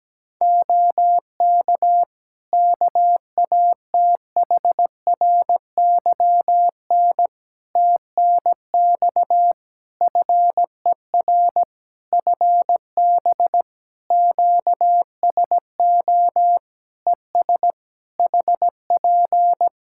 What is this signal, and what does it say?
Morse code, heard as a keyed tone, about 17 wpm, 700 Hz. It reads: OK KATHRYN TNX FER FB QSO ES HP
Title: K1USN SST QSO Practice 17wpm